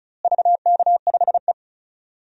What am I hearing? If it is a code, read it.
Morse code, heard as a keyed tone, about 36 wpm, 700 Hz. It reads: VK5E
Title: DXCC-100 Call Signs - 4 Characters - Repeated 2x - 35wpm